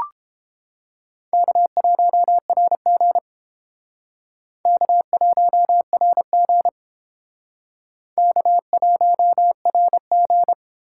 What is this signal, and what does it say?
Morse code, heard as a keyed tone, about 30 wpm, 700 Hz. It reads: K1RG K1RG K1RG